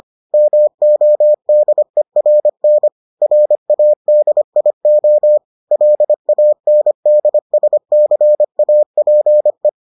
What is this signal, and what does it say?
Morse code, heard as a keyed tone, about 25 wpm, 600 Hz. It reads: MODERN RADIO LANDSCAPE